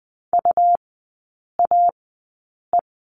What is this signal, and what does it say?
Morse code, heard as a keyed tone, about 20 wpm, spaced as for 8 wpm, 700 Hz. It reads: UAE
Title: 121 Lesson 20e - Number 3 - Three Characters 20wpm